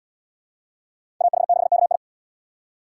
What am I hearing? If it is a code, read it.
Morse code, heard as a keyed tone, about 68 wpm, 700 Hz. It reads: AH6KI